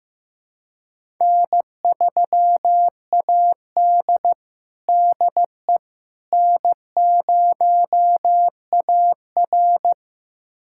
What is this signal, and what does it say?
Morse code, heard as a keyed tone, about 15 wpm, 700 Hz. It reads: N3AD DE N0AR
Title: USA Callsign DE Callsign - 4 Characters - 15wpm